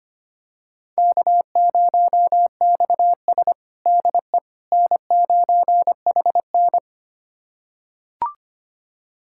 Morse code, heard as a keyed tone, about 25 wpm, 700 Hz. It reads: K0XS DE N9HN
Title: USA Callsign DE Callsign - 4 Characters - 25wpm